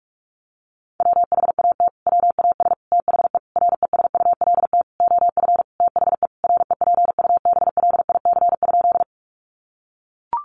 Morse code, heard as a keyed tone, about 44 wpm, 700 Hz. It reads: WHAT WAS THE RESULT OF THE REPUBLIC?
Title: Sentences from Top 1500 Words 45wpm